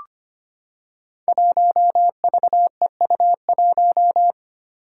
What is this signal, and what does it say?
Morse code, heard as a keyed tone, about 25 wpm, 700 Hz. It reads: 1VEU1
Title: Five Challenging Letter-Number Combinations 25wpm